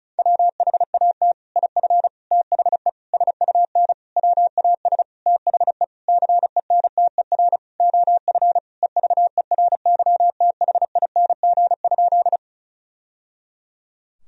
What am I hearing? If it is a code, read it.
Morse code, heard as a keyed tone, about 34 wpm, 700 Hz. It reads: WHAT IF THE SUN WAS THE CENTER OF EVERYTHING?
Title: Sentences from Top 1000 Words - Review Part 2 - 35wpm